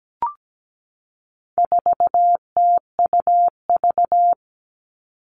Morse code, heard as a keyed tone, about 17 wpm, 700 Hz. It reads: 4TUV
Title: All Cut Numbers - 4-Digit 17wpm